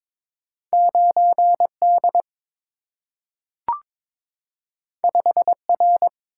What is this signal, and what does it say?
Morse code, heard as a keyed tone, about 23 wpm, 690 Hz. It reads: 9D 5R
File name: practice / 22wpm / Two Letter-Number Combinations - 22wpm